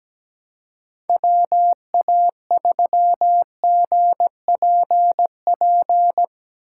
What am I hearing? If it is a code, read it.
Morse code, heard as a keyed tone, about 17 wpm, 700 Hz. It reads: WA3GPP
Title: K1USN SST QSO Practice - 17wpm